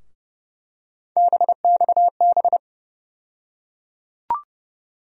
Morse code, heard as a keyed tone, about 30 wpm, 700 Hz. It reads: BXB
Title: USA Call Sign Postfixes - 3 Letters 30wpm